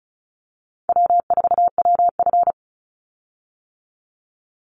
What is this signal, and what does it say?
Morse code, heard as a keyed tone, about 34 wpm, 700 Hz. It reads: W4WF
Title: Call Signs - 35wpm